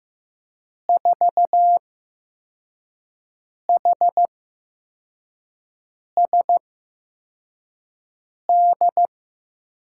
Morse code, heard as a keyed tone, about 15 wpm, spaced as for 4 wpm, 700 Hz. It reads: 4HSD
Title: USA Call Signs - 6 Characters - 8x Character Spacing - 15wpm